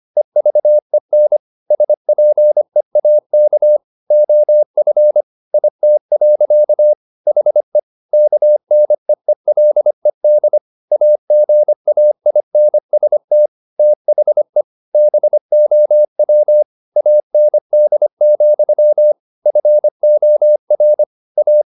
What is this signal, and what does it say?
Morse code, heard as a keyed tone, about 25 wpm, 600 Hz. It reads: EVEN SPEAK OF IT. HE KNEELED AGAINST THE BOW AND, FOR A